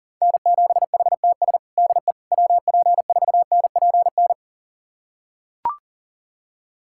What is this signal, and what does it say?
Morse code, heard as a keyed tone, about 37 wpm, 700 Hz. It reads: N7HTS DE WP4NPN
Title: USA Callsign DE Callsign 40wpm